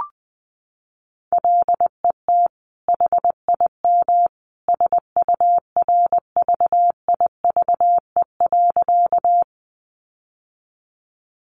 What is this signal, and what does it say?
Morse code, heard as a keyed tone, about 19 wpm, 700 Hz. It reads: LET HIM SURVIVE.
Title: Sentences from Top 1300 Words 20wpm